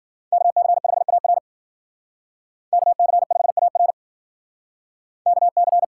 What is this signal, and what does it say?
Morse code, heard as a keyed tone, about 60 wpm, 700 Hz. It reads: KC5AR KC5AR KC